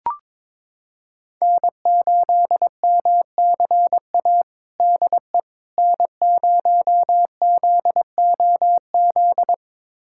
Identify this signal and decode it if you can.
Morse code, heard as a keyed tone, about 22 wpm, 700 Hz. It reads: N8MCA DE N0ZOZ
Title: USA Callsign DE Callsign - 5 Characters - 22wpm